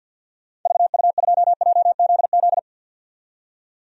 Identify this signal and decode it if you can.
Morse code, heard as a keyed tone, about 51 wpm, 700 Hz. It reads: VU2JZZ
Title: DXCC-100 Call Signs - 6 Characters - 50wpm